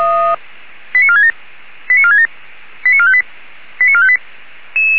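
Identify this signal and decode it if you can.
Five-tone selective call (ZVEI), heard as quick runs of five tones, tones of about 70 ms. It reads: 89467 89467 89458 89458
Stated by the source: ZVEI five-tone call 89356 received by SDR, recorded by feuerwehr-orlamuende (f85360) fire brigade alerting server.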